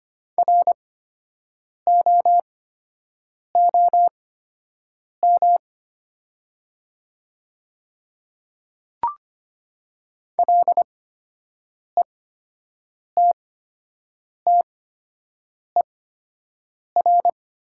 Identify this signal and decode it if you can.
Morse code, heard as a keyed tone, about 25 wpm, spaced as for 7 wpm, 700 Hz. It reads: ROOM LETTER
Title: 8x Character Spacing - Top 300 Words - 25wpm